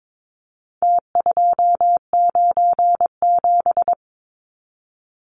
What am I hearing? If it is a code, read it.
Morse code, heard as a keyed tone, about 22 wpm, 700 Hz. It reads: T297
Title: Numbers - Contest Style - 4-Digit 22wpm